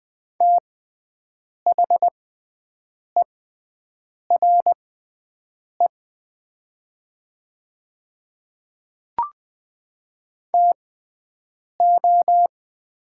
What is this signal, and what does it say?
Morse code, heard as a keyed tone, about 21 wpm, spaced as for 7 wpm, 700 Hz. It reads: THERE TO
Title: 6x Character Spacing - Top 200 Words - 20wpm